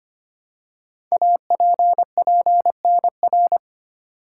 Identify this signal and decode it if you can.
Morse code, heard as a keyed tone, about 25 wpm, 700 Hz. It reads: APPNR
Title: Code Groups - 5 Letter:Numbers - 25wpm